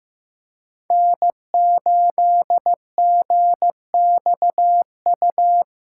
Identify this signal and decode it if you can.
Morse code, heard as a keyed tone, about 18 wpm, 700 Hz. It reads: N8GXU
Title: Speed Racing - USA 1x3 Call Signs 15wpm